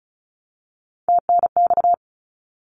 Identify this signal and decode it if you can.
Morse code, heard as a keyed tone, about 34 wpm, 700 Hz. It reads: TNX
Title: Abbreviations Repeated 3x - 35wpm